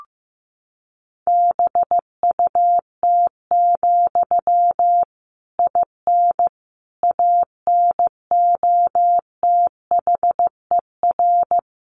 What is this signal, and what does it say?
Morse code, heard as a keyed tone, about 15 wpm, 700 Hz. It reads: BUT, IN ANOTHER
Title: Sentences from Top 1500 Words 15wpm